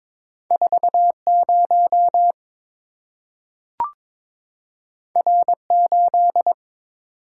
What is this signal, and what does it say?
Morse code, heard as a keyed tone, about 22 wpm, 700 Hz. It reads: 40 R8
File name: practice / 22wpm / Two Letter-Number Combinations - 22wpm